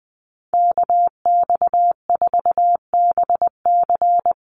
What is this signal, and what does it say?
Morse code, heard as a keyed tone, about 20 wpm, 700 Hz. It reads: KX4BC
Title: USA Call Signs Repeated 2x 20wpm